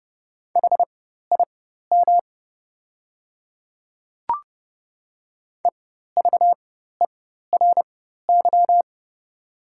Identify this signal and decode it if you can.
Morse code, heard as a keyed tone, about 30 wpm, spaced as for 14 wpm, 700 Hz. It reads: HIM EVERY
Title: 4x Character Spacing - Top 200 Words - 30wpm